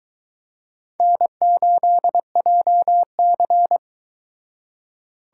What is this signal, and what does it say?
Morse code, heard as a keyed tone, about 24 wpm, 700 Hz. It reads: N8JC